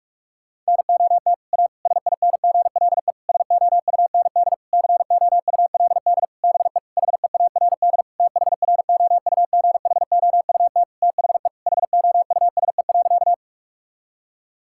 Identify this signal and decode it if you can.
Morse code, heard as a keyed tone, about 46 wpm, 700 Hz. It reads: NOT A SINGLE SOUND COULD BE HEARD THROUGHOUT THE HOUSE.
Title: Sentences from Top 1000 Words - Review Part 5 - 45wpm